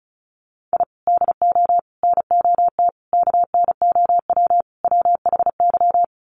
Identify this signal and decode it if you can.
Morse code, heard as a keyed tone, about 34 wpm, 700 Hz. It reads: I DO NOT KNOW WHY
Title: Sets of 5 Words 35wpm